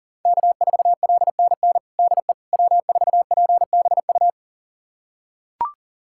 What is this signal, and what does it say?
Morse code, heard as a keyed tone, about 40 wpm, 700 Hz. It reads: K4LNN DE W4PBU
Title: USA Callsign DE Callsign - 5 Characters - 40wpm